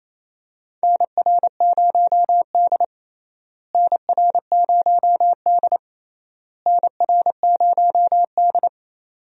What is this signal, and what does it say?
Morse code, heard as a keyed tone, about 28 wpm, 700 Hz. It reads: NR0D NR0D NR0D
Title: USA Call Signs - 4 Characters - Repeated 3x - 28wpm